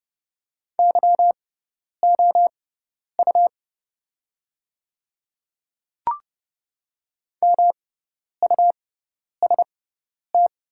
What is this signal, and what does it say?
Morse code, heard as a keyed tone, about 30 wpm, spaced as for 10 wpm, 700 Hz. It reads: YOU MUST
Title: 6x Character Spacing - Top 200 Words - 30wpm